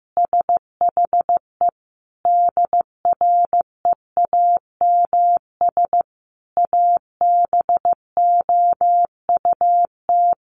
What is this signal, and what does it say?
Morse code, heard as a keyed tone, about 15 wpm, 700 Hz. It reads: SHE DREAMS ABOUT
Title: Easy Phrases 15wpm